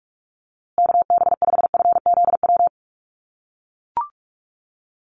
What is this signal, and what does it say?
Morse code, heard as a keyed tone, about 44 wpm, 700 Hz. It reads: KB5FZW
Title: USA Call Signs Repeated 3x 45wpm